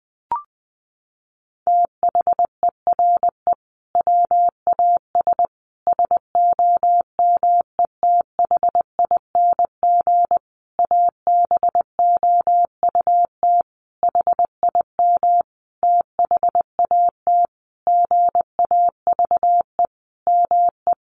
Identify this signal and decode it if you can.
Morse code, heard as a keyed tone, about 19 wpm, 700 Hz. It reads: THERE WAS SOMETHING ABOUT HIM THAT GAVE ME
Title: Sentences from Top 1500 Words 20wpm